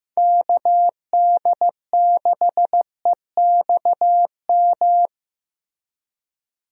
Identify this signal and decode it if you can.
Morse code, heard as a keyed tone, about 15 wpm, 700 Hz. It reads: KD6EXM
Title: USA Call Signs - 6 Characters - 15wpm